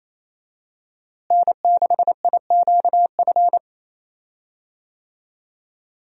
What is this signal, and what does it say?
Morse code, heard as a keyed tone, about 28 wpm, 700 Hz. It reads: N6IQF
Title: USA Call Signs - 5 Characters - Repeated 3x - 28wpm